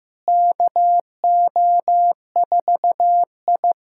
Morse code, heard as a keyed tone, about 15 wpm, 700 Hz. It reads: KO4I
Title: USA Callsign DE Callsign - 4 Characters - 15wpm